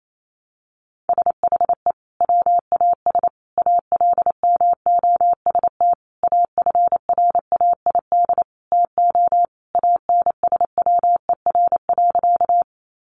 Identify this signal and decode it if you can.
Morse code, heard as a keyed tone, about 28 wpm, 700 Hz. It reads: SHE WAS ALMOST AFRAID TO ANSWER.
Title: Sentences from Top 1500 Words 28wpm